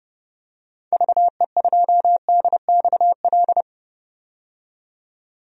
Morse code, heard as a keyed tone, about 31 wpm, 700 Hz. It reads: VE2DXL
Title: DXCC-100 Call Signs - 6 Characters - 30wpm